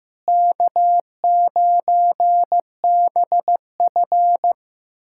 Morse code, heard as a keyed tone, about 15 wpm, 700 Hz. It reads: K9BF
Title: USA Callsign DE Callsign - 4 Characters - 15wpm